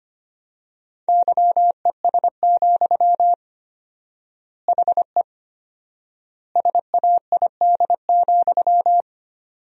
Morse code, heard as a keyed tone, about 25 wpm, 700 Hz. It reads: YES, HE SAID,
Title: 4x Word-Spacing - Sentences from Top 500 Words - 25wpm